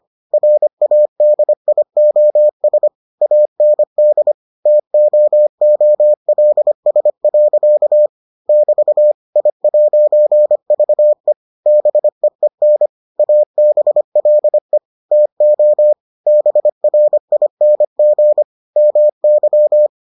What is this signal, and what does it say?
Morse code, heard as a keyed tone, about 25 wpm, 600 Hz. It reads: RADIOS AND TOOLS. <BT> I'VE BEEN ABLE TO BRING MY